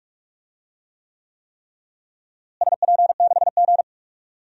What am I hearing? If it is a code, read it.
Morse code, heard as a keyed tone, about 46 wpm, 700 Hz. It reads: SP6G
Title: DXCC-100 Call Signs - 4 Characters - Repeated 3x - 45wpm